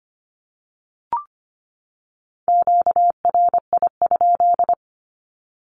Morse code, heard as a keyed tone, about 25 wpm, 700 Hz. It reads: QRI?
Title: Q-Codes 25wpm